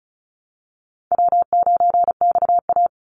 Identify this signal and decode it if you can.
Morse code, heard as a keyed tone, about 34 wpm, 700 Hz. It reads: W9XA
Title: Call Signs - 35wpm